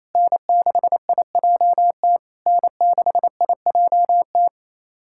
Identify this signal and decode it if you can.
Morse code, heard as a keyed tone, about 28 wpm, 700 Hz. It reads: N6IJT N6IJT
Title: SOTA QSO Practice 28wpm